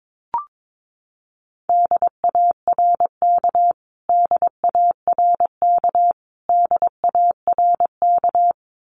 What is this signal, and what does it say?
Morse code, heard as a keyed tone, about 22 wpm, 700 Hz. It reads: DARK DARK DARK
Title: Top 500 Words Repeated 22wpm